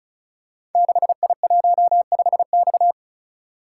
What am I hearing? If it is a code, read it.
Morse code, heard as a keyed tone, about 36 wpm, 690 Hz. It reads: 6I15X
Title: Code Groups - 5 Letter:Numbers - 35wpm